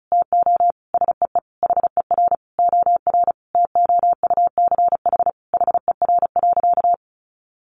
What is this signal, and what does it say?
Morse code, heard as a keyed tone, about 34 wpm, 700 Hz. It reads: TO SEE HER OR TOUCH HER.
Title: Sentences from Top 1200 Words 35wpm